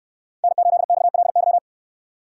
Morse code, heard as a keyed tone, about 68 wpm, 700 Hz. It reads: N8XLY